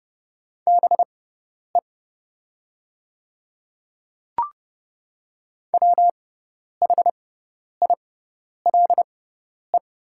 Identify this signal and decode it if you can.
Morse code, heard as a keyed tone, about 30 wpm, spaced as for 10 wpm, 700 Hz. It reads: BE WHILE
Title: 6x Character Spacing - Top 300 Words - 30wpm